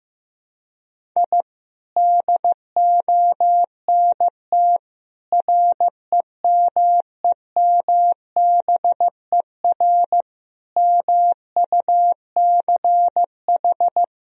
Morse code, heard as a keyed tone, about 15 wpm, 700 Hz. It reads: I DONT REMEMBER MUCH
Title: Sentences from Top 1000 Words - Review Part 1 - 15wpm